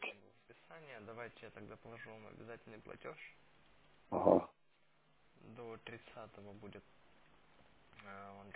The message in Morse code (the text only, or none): none